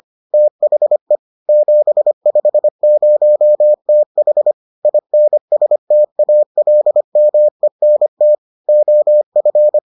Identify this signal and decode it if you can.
Morse code, heard as a keyed tone, about 25 wpm, 600 Hz. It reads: THE 750TH INSTALMENT OF